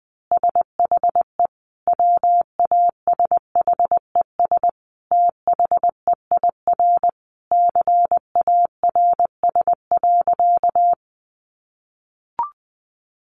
Morse code, heard as a keyed tone, about 20 wpm, 700 Hz. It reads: SHE WASHES THEIR CARS.
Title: Easy Phrases 20wpm